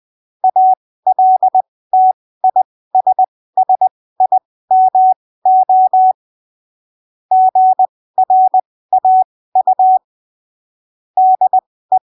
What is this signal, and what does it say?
Morse code, heard as a keyed tone, about 20 wpm, spaced as for 15 wpm, 750 Hz. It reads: ALTISSIMO GRAU DE
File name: arquivos / texto / texto 11 20ppm